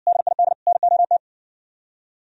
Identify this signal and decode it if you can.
Morse code, heard as a keyed tone, about 63 wpm, 700 Hz. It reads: DID NOT